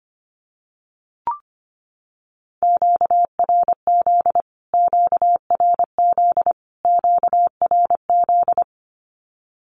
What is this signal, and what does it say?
Morse code, heard as a keyed tone, about 25 wpm, 700 Hz. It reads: QRZ QRZ QRZ
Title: Q-Codes Repeated 3x 25wpm